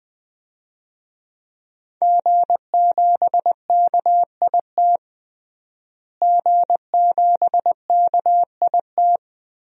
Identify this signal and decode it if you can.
Morse code, heard as a keyed tone, about 21 wpm, 700 Hz. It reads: G7KIT G7KIT
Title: DXCC-100 Call Signs - 5 Characters - Repeated 2x - 20wpm